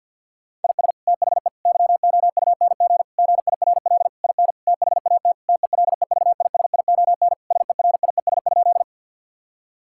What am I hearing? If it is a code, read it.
Morse code, heard as a keyed tone, about 52 wpm, 700 Hz. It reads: IS THE YOUNG GIRL IN THAT TELEVISION SERIES?
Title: Sentences from Top 1000 Words - Review Part 5 - 50wpm